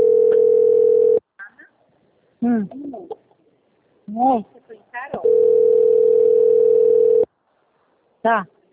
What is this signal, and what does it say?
Ringback tone (North America), 440 + 480 Hz, with speech.
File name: out- - - - - .28324